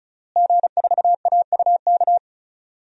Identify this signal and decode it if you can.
Morse code, heard as a keyed tone, about 34 wpm, 700 Hz. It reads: G4AUK
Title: DXCC-100 Call Signs - 5 Characters - Repeated 3x - 35wpm